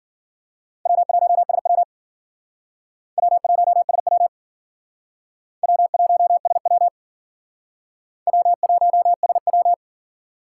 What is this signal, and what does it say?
Morse code, heard as a keyed tone, about 54 wpm, 700 Hz. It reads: W1SW W1SW W1SW W1SW